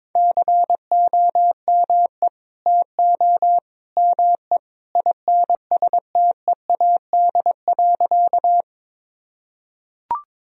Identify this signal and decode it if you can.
Morse code, heard as a keyed tone, about 22 wpm, 700 Hz. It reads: COME TO ME INSTEAD.
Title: Sentences from Top 1000 Words - Review Part 4 - 22wpm